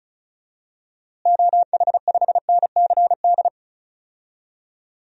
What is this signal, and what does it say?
Morse code, heard as a keyed tone, about 36 wpm, 700 Hz. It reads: OH5NCD
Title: DXCC-100 Call Signs - 6 Characters - Repeated 3x - 35wpm